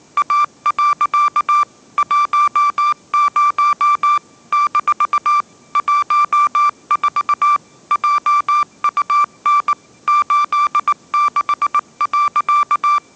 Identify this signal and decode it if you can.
Morse code, heard as a keyed tone, about 19 wpm, 1180 Hz. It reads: A. 10 - 14 JUN 86.
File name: mrx083 20wpm